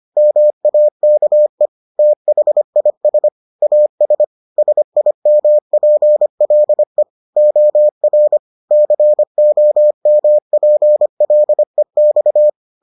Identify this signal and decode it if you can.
Morse code, heard as a keyed tone, about 25 wpm, 600 Hz. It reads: MAKE THIS AS SIMPLE OR COMPLEX